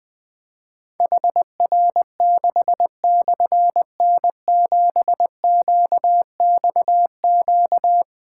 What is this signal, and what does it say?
Morse code, heard as a keyed tone, about 21 wpm, 700 Hz. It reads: HR6/N7QXQ
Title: Difficult Call Signs 20wpm